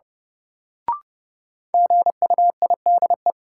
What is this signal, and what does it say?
Morse code, heard as a keyed tone, about 30 wpm, 700 Hz. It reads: GUIDE
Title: Sentences from Top 600 Words - New Word Drill - 30wpm